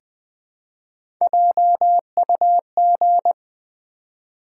Morse code, heard as a keyed tone, about 20 wpm, 700 Hz. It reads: JUG
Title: Easy Words with Hints - 20wpm